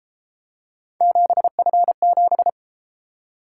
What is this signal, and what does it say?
Morse code, heard as a keyed tone, about 31 wpm, 700 Hz. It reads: 7F7